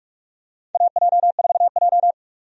Morse code, heard as a keyed tone, about 46 wpm, 700 Hz. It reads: AJ4J
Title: USA Call Signs - 4 Characters - 45wpm